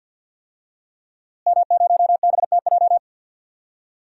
Morse code, heard as a keyed tone, about 46 wpm, 700 Hz. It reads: M0BTJ
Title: DXCC-100 Call Signs - 5 Characters - Repeated 3x - 50wpm